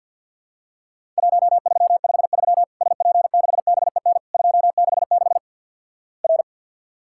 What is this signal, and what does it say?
Morse code, heard as a keyed tone, about 50 wpm, 700 Hz. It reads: 1353 SP6BEN 266 R
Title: Worked All Europe DX Contest - QTC Exchange with Repeats - 50wpm